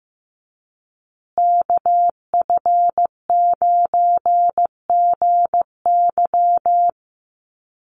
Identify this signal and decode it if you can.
Morse code, heard as a keyed tone, about 15 wpm, 700 Hz. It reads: KF9GY
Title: USA Call Signs Repeated 2x 15wpm